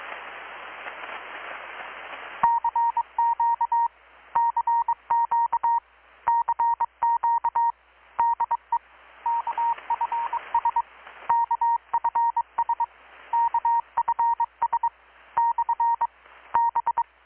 Morse code, heard as a keyed tone, about 21 wpm, 930 Hz. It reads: CQ CQ CQ DE KFS KFS KFS / B
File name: blog / swl / 6-abril-2019-12695-USB-KPS